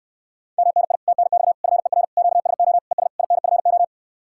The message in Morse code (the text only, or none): MNI TNX FER QSO ES INFO